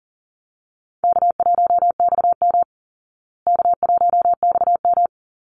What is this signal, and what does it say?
Morse code, heard as a keyed tone, about 40 wpm, 700 Hz. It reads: K1XM K1XM
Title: Easy Call Signs Repeated 2x - 40wpm